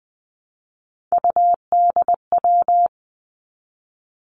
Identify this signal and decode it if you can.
Morse code, heard as a keyed tone, about 20 wpm, 700 Hz. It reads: UDW
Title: Three Challenging Letter-Number Combinations 20wpm